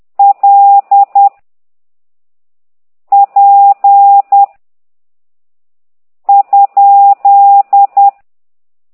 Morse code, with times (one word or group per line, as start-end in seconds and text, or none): 0.2-1.3 L
3.1-4.4 P
6.3-8.1 ?